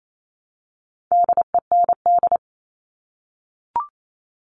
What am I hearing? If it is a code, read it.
Morse code, heard as a keyed tone, about 28 wpm, 700 Hz. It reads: DEND
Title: All Cut Numbers - 4-Digit 28wpm